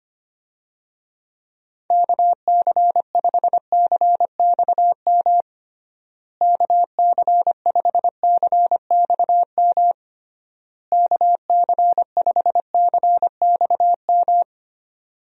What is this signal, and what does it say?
Morse code, heard as a keyed tone, about 25 wpm, 700 Hz. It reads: KC5CXM KC5CXM KC5CXM
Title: USA Call Signs - 6 Characters - Repeated 3x - 25wpm